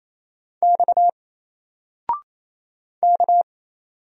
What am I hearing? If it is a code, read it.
Morse code, heard as a keyed tone, about 27 wpm, 700 Hz. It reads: X K
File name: Single-Letter-Number-28wpm